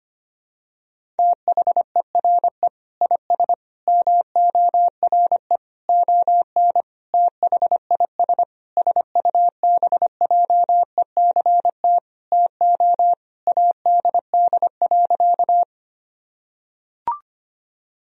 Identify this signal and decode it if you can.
Morse code, heard as a keyed tone, about 25 wpm, 700 Hz. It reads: THERE IS MORE ON THIS SUBJECT TO ADD.
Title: Sentences from Top 1000 Words - Review Part 1 - 25wpm